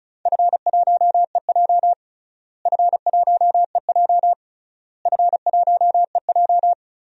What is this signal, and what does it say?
Morse code, heard as a keyed tone, about 36 wpm, 700 Hz. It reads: F1EJ F1EJ F1EJ
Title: DXCC-100 Call Signs - 4 Characters - Repeated 3x - 35wpm